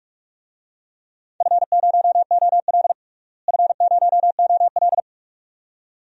Morse code, heard as a keyed tone, about 46 wpm, 700 Hz. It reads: F0OL F0OL
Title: DXCC-100 Call Signs - 4 Characters - Repeated 2x - 45wpm